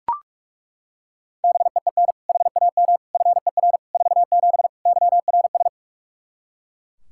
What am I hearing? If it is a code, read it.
Morse code, heard as a keyed tone, about 41 wpm, 700 Hz. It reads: BEEN HAM FER 47 YRS
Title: Practice QSO - Exchange 3 - More Info - 45wpm